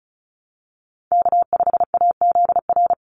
Morse code, heard as a keyed tone, about 34 wpm, 700 Hz. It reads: K5AZR
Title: Call Signs Repeated 3x - 35wpm